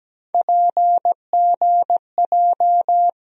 Morse code, heard as a keyed tone, about 17 wpm, 700 Hz. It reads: PGJ
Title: USA Call Sign Postfixes - 3 Letters 17wpm